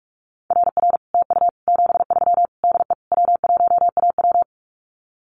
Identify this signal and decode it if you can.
Morse code, heard as a keyed tone, about 44 wpm, 700 Hz. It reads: RR TU 73 DE W1AW
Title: Common QSO Phrases Encore 45wpm